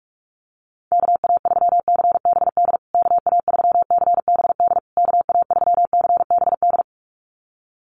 Morse code, heard as a keyed tone, about 44 wpm, 700 Hz. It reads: KA3CBD KA3CBD KA3CBD
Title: USA Call Signs Repeated 3x 45wpm